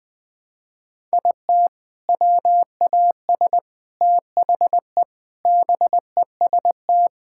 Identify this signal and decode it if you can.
Morse code, heard as a keyed tone, about 21 wpm, 700 Hz. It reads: IT WAS THE BEST
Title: Sentences with 8-Word Phrases - 20wpm